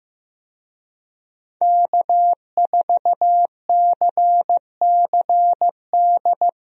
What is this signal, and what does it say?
Morse code, heard as a keyed tone, about 16 wpm, 700 Hz. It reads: K4CCD
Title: USA Call Signs - 5 Characters - 15wpm